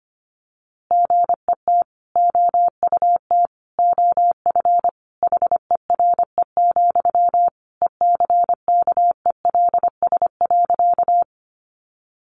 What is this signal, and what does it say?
Morse code, heard as a keyed tone, about 25 wpm, 700 Hz. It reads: GET OUT OF HERE, ECKELS.
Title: A Sound Of Thunder 25wpm